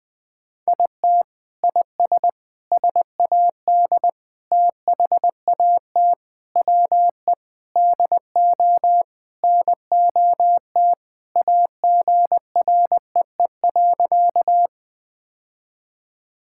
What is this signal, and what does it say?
Morse code, heard as a keyed tone, about 20 wpm, 700 Hz. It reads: IT IS SAD THAT WE DO NOT AGREE.
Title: Sentences with 4-Word Phrases - Spoken Beginning - 20wpm